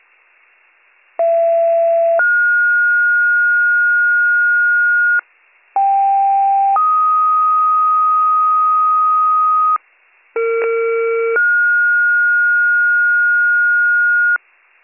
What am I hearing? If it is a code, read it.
Two-tone page, heard as a one-second tone, then a three-second tone, about 670 then 1430 Hz, about 790 then 1250 Hz, about 460 then 1430 Hz, with noise. It is heard from 1.2 to 14.4 s.